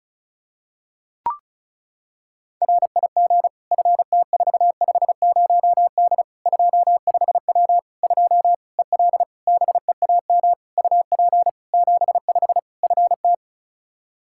Morse code, heard as a keyed tone, about 36 wpm, 700 Hz. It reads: RIG FT450D 25W 2 EL BEAM UP 75 FT
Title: Practice QSO - Exchange 2 - Info - 35wpm